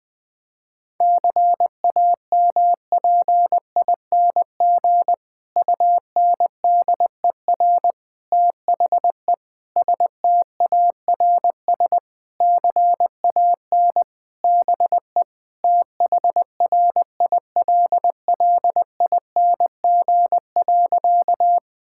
Morse code, heard as a keyed tone, about 21 wpm, 700 Hz. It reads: CAMPING UNDER THE STARS CAN BE THRILLING.
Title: Difficult Sentences with Hints - 20wpm